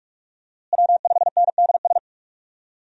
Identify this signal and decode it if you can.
Morse code, heard as a keyed tone, about 42 wpm, 700 Hz. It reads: W5NDS
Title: USA Call Signs - 5 Characters - Repeated 2x - 45wpm